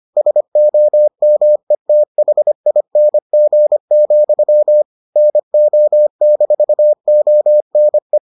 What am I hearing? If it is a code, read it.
Morse code, heard as a keyed tone, about 25 wpm, 600 Hz. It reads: SOMETHING, NO-ONE